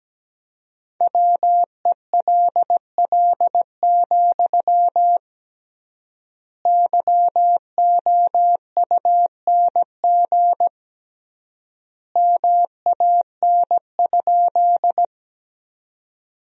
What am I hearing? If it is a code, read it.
Morse code, heard as a keyed tone, about 17 wpm, 700 Hz. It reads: WELL, YOUNG MAN?
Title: 3x Word-Spacing - Sentences from Top 400 Words - 17wpm